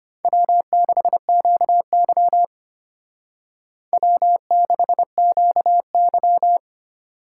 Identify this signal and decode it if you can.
Morse code, heard as a keyed tone, about 30 wpm, 700 Hz. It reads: W6QY W6QY
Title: Speed Racing - USA 1x2 Call Signs 25wpm